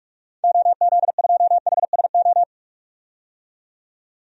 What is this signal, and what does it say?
Morse code, heard as a keyed tone, about 42 wpm, 700 Hz. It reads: OZ2HSO
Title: DXCC-100 Call Signs - 6 Characters - 45wpm